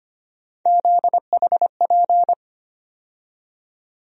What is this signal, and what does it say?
Morse code, heard as a keyed tone, about 25 wpm, 700 Hz. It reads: ZHP
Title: USA Call Sign Postfixes - 3 Letters 25wpm